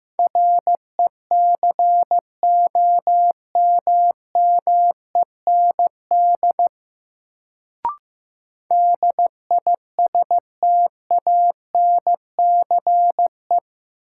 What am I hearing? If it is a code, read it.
Morse code, heard as a keyed tone, about 15 wpm, 700 Hz. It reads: RECOMMEND DISTANCE
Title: Sentences from Top 1200 Words - New Word Drill - 15wpm